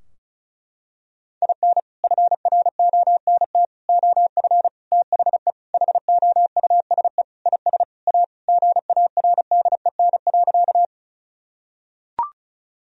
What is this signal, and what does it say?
Morse code, heard as a keyed tone, about 36 wpm, 700 Hz. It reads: IN FRONT OF THE HOUSE IS A GARDEN.
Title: Sentences from Top 1000 Words - Review Part 4 - 35wpm